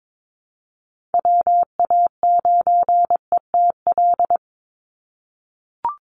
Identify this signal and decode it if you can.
Morse code, heard as a keyed tone, about 22 wpm, 700 Hz. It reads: WA9ETL
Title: USA Call Signs 22wpm